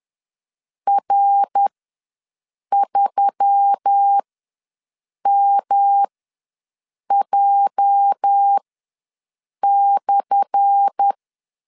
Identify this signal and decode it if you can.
Morse code, heard as a keyed tone, about 10 wpm, spaced as for 6 wpm, 800 Hz. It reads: R3MJ/
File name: LEZIONI / 10 SIGNS / 10B INT ws 10wpm 4134